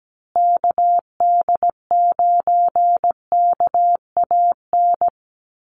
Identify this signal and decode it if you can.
Morse code, heard as a keyed tone, about 17 wpm, 700 Hz. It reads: KD9KAN
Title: USA Call Signs Repeated 3x 17wpm